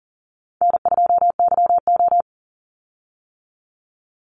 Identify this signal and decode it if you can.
Morse code, heard as a keyed tone, about 42 wpm, 700 Hz. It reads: N2YO
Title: Easy Call Signs - 40wpm